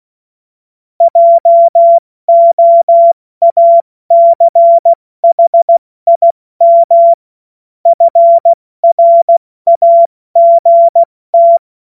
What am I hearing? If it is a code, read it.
Morse code, heard as a keyed tone, about 16 wpm, 680 Hz. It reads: JOACHIM FRAGT
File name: Morsekurs / Witze / Witz 28. März 2023 70 F80